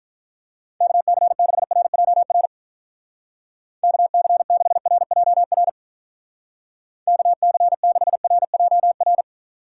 Morse code, heard as a keyed tone, about 46 wpm, 700 Hz. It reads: KC6RJR KC6RJR KC6RJR